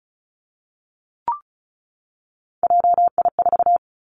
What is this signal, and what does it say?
Morse code, heard as a keyed tone, about 36 wpm, 700 Hz. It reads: JI4
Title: Three Challenging Letter-Number Combinations 35wpm